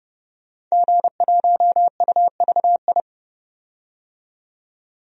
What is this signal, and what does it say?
Morse code, heard as a keyed tone, about 30 wpm, 700 Hz. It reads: G1UVI
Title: DXCC-100 Call Signs - 5 Characters - 30wpm